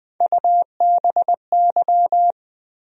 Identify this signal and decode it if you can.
Morse code, heard as a keyed tone, about 20 wpm, 700 Hz. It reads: UBY
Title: USA Call Sign Postfixes - 3 Letters 20wpm